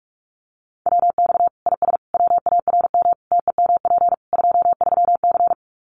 Morse code, heard as a keyed tone, about 44 wpm, 700 Hz. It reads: WX IS WARM TEMP 23C
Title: Common QSO Phrases 45wpm